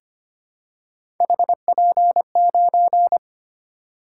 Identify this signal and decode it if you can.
Morse code, heard as a keyed tone, about 25 wpm, 700 Hz. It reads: HP9
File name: Three Letter-Number Combinations - 25wpm